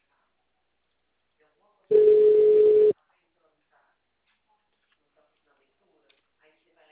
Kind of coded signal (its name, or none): telephone call progress tone